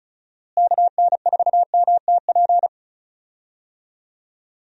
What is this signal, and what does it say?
Morse code, heard as a keyed tone, about 36 wpm, 700 Hz. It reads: KN4MTP
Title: USA Call Signs - 6 Characters - Repeated 3x - 35wpm